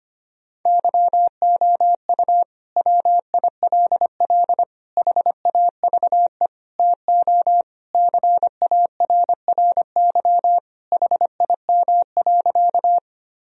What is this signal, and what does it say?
Morse code, heard as a keyed tone, about 25 wpm, 700 Hz. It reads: YOU WILL HAVE TO CARRY HIM.
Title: Sentences from Top 1000 Words - Review Part 1 - 25wpm